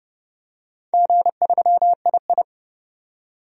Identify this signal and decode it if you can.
Morse code, heard as a keyed tone, about 30 wpm, 700 Hz. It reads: G3II
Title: DXCC-100 Call Signs - 4 Characters - 30wpm